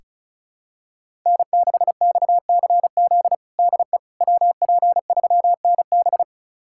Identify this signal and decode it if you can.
Morse code, heard as a keyed tone, about 36 wpm, 700 Hz. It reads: N6XCZ DE WP3NB
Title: USA Callsign DE Callsign - 5 Characters - 35wpm